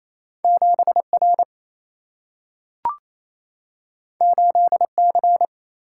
Morse code, heard as a keyed tone, about 28 wpm, 700 Hz. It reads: 7R 8C
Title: Single-Letter Single-Number 28wpm